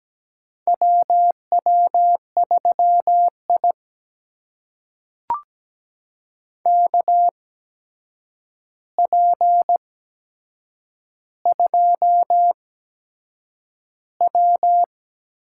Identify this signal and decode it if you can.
Morse code, heard as a keyed tone, about 17 wpm, spaced as for 7 wpm, 700 Hz. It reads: WW3I KP2W
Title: USA Call Signs - 4 Characters - 8x Character Spacing - 17wpm